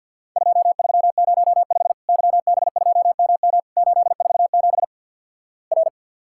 Morse code, heard as a keyed tone, about 50 wpm, 700 Hz. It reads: 2305 YB2MM 847 R
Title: Worked All Europe DX Contest - QTC Exchange with Repeats - 50wpm